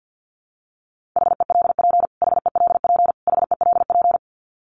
Morse code, heard as a keyed tone, about 50 wpm, 700 Hz. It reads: HELP HELP HELP
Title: Top 300 Words Repeated 50wpm